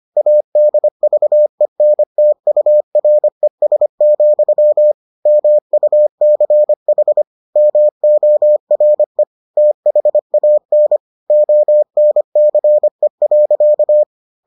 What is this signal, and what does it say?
Morse code, heard as a keyed tone, about 25 wpm, 600 Hz. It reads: ADVENTURES, MUCH MORE THAN ONCE.